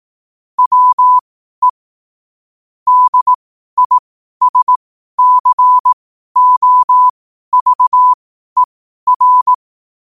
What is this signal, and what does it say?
Morse code, heard as a keyed tone, about 18 wpm, spaced as for 12 wpm, 1000 Hz. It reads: WE DISCOVER
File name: qotdfiles / 12wpm